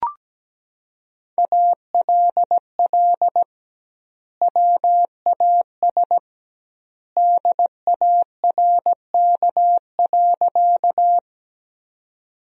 Morse code, heard as a keyed tone, about 17 wpm, 700 Hz. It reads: ALL WAS DARK.
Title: 2x Word-Spacing - Sentences from Top 500 Words - 17wpm